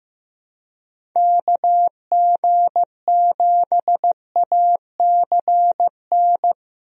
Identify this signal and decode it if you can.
Morse code, heard as a keyed tone, about 15 wpm, 700 Hz. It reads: KG7ACN
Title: USA Call Signs - 6 Characters - 15wpm